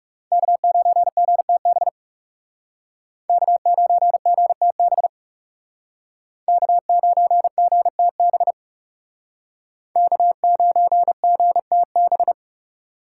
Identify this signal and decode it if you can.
Morse code, heard as a keyed tone, about 38 wpm, 700 Hz. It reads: K9GTB K9GTB K9GTB K9GTB